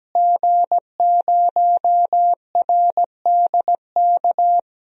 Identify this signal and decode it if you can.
Morse code, heard as a keyed tone, about 17 wpm, 700 Hz. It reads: G0RDK
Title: DXCC-100 Call Signs - 5 Characters - Repeated 2x - 17wpm